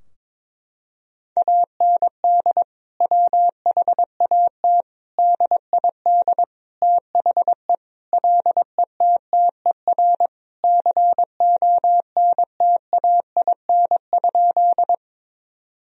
Morse code, heard as a keyed tone, about 22 wpm, 700 Hz. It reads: AND WHAT DID THE LETTER CONTAIN?
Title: Sentences from Top 1000 Words - Review Part 2 - 22wpm